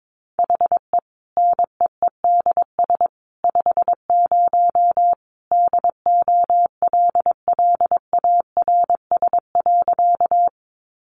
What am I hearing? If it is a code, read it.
Morse code, heard as a keyed tone, about 22 wpm, 700 Hz. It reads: HE NEEDS 50 DOLLARS.
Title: Additional Easy Phrases 22wpm